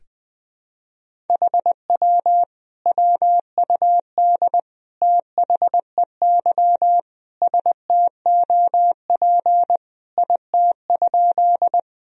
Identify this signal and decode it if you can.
Morse code, heard as a keyed tone, about 21 wpm, 700 Hz. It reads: HW WUD THEY STOP IT?
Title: Sentences from Top 300 Words and Abbreviations 20wpm